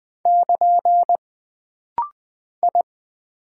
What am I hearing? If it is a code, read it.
Morse code, heard as a keyed tone, about 20 wpm, 700 Hz. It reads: <KN> I
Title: Single Character (Letter:Number:Punctuation:Prosign) - Rapid-Fire - 20wpm